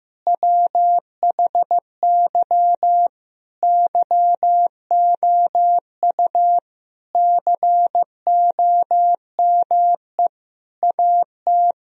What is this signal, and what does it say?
Morse code, heard as a keyed tone, about 16 wpm, 700 Hz. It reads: WHY YOU COME AT
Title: Sentences from Top 1000 Words - Review Part 2 - 15wpm